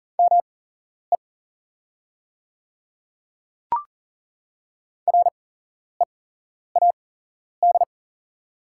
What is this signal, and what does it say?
Morse code, heard as a keyed tone, about 41 wpm, spaced as for 11 wpm, 700 Hz. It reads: ME READ
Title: 8x Character Spacing - Top 300 Words - 40wpm